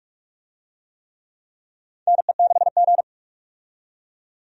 Morse code, heard as a keyed tone, about 46 wpm, 700 Hz. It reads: NE6G
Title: USA Call Signs - 4 Characters - 45wpm